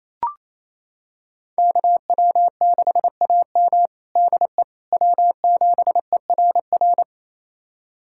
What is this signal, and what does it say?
Morse code, heard as a keyed tone, about 28 wpm, 700 Hz. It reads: KW6AM DE W7ERR
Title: USA Callsign DE Callsign - 5 Characters - 28wpm